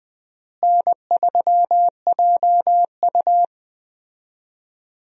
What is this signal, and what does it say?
Morse code, heard as a keyed tone, about 20 wpm, 700 Hz. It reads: N3JU
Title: USA Call Signs - 4 Characters - Repeated 3x - 20wpm